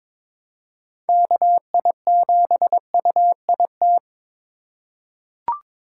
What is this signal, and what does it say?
Morse code, heard as a keyed tone, about 21 wpm, 700 Hz. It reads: KI7UIT
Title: USA Call Signs - 6 Characters - 2x Character Spacing - 22wpm